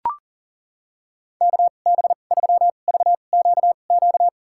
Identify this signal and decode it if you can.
Morse code, heard as a keyed tone, about 40 wpm, spaced as for 29 wpm, 700 Hz. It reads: KB3VQQ
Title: USA Call Signs - 6 Characters - 2x Character Spacing - 40wpm